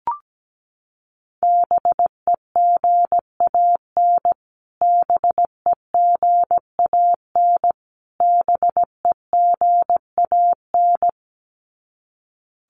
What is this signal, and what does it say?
Morse code, heard as a keyed tone, about 17 wpm, 700 Hz. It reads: BEGAN BEGAN BEGAN
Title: Top 500 Words Repeated 17wpm